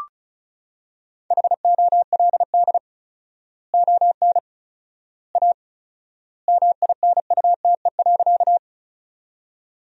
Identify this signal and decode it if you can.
Morse code, heard as a keyed tone, about 34 wpm, 700 Hz. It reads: HOLD ON A MINUTE.
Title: 4x Word-Spacing - Sentences from Top 500 Words - 35wpm